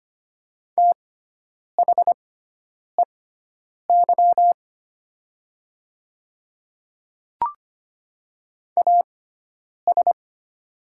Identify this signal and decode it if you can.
Morse code, heard as a keyed tone, about 25 wpm, spaced as for 9 wpm, 700 Hz. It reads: THEY AS
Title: 6x Character Spacing - Top 100 Words - 25wpm